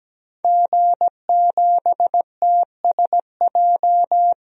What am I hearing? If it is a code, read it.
Morse code, heard as a keyed tone, about 17 wpm, 700 Hz. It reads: G7TSJ
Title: DXCC-100 Call Signs - 5 Characters - 17wpm